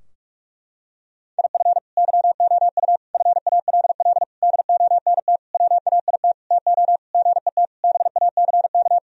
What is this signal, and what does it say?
Morse code, heard as a keyed tone, about 41 wpm, 700 Hz. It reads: IF YOU FALL DONT WAIT TO GET BACK
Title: Sentences from Top 1000 Words - Review Part 2 - 45wpm